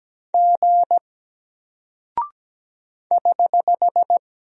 Morse code, heard as a keyed tone, about 17 wpm, 700 Hz. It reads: G <HH>
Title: Single Character (Letter:Number:Punctuation:Prosign) Alt 17wpm